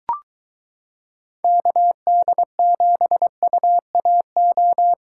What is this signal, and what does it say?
Morse code, heard as a keyed tone, about 23 wpm, 700 Hz. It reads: KD7UAO